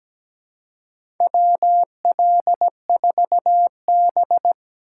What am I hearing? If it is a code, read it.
Morse code, heard as a keyed tone, about 17 wpm, 700 Hz. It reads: WL4B
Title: USA Call Signs - 4 Characters - 4x Character Spacing - 17wpm